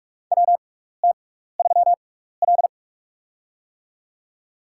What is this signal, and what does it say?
Morse code, heard as a keyed tone, about 46 wpm, spaced as for 16 wpm, 700 Hz. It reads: WT3L
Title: USA Call Signs - 4 Characters - 6x Character Spacing - 45wpm